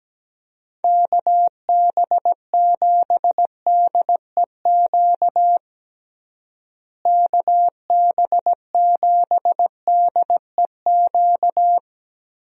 Morse code, heard as a keyed tone, about 17 wpm, 700 Hz. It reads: KB7DEQ KB7DEQ
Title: USA Call Signs - 6 Characters - Repeated 2x - 17wpm